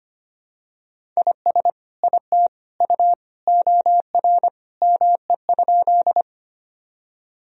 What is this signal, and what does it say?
Morse code, heard as a keyed tone, about 25 wpm, 700 Hz. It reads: IS IT U OR ME?
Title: Sentences from Top 200 Words and Abbreviations 25wpm